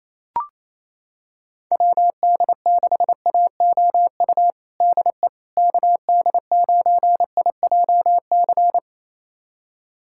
Morse code, heard as a keyed tone, about 28 wpm, 700 Hz. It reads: WD6AOU DE KD9IJC
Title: USA Callsign DE Callsign - 6 Characters - 28wpm